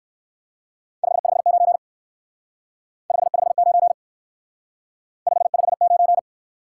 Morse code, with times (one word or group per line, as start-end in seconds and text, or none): 1.0-1.8 559
3.1-3.9 559
5.3-6.2 559